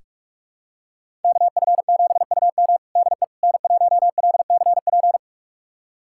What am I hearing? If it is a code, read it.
Morse code, heard as a keyed tone, about 47 wpm, 700 Hz. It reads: KF7UM DE N1LCP
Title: USA Callsign DE Callsign - 5 Characters - 45wpm